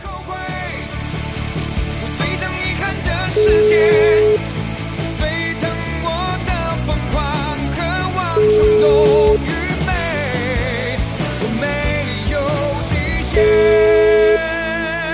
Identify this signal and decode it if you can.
Ringback tone (Europe, 425 Hz), with noise.